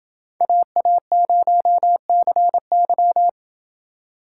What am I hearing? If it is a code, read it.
Morse code, heard as a keyed tone, about 28 wpm, 700 Hz. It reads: AA0CY